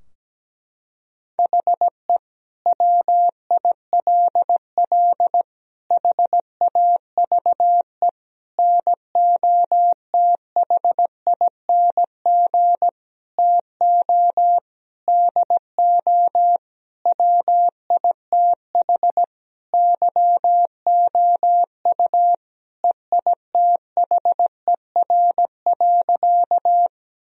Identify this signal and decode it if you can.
Morse code, heard as a keyed tone, about 17 wpm, 700 Hz. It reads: HE WILL HAVE NOTHING TO DO WITH YOU EITHER.
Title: Sentences from Top 1000 Words - Review Part 4 - 17wpm